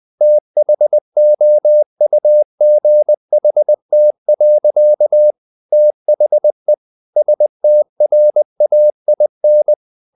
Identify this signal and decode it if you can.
Morse code, heard as a keyed tone, about 20 wpm, 600 Hz. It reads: THOUGHT. THE STRAIN
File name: the old man and the sea part 2 20wpm